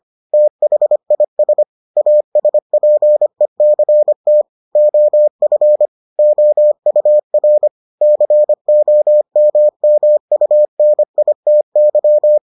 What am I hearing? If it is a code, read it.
Morse code, heard as a keyed tone, about 25 wpm, 600 Hz. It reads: THIS ASPECT OF OUR COMMUNITY